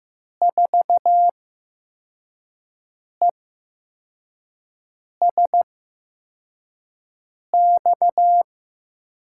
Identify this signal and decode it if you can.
Morse code, heard as a keyed tone, about 15 wpm, spaced as for 4 wpm, 700 Hz. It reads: 4ESX
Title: USA Call Signs - 5 Characters - 8x Character Spacing - 15wpm